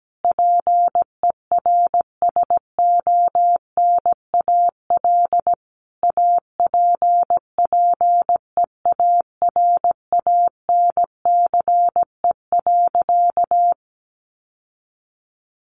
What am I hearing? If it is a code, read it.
Morse code, heard as a keyed tone, about 17 wpm, 700 Hz. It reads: PERSONAL APPEARANCE.
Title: Sentences from Top 1100 Words 17wpm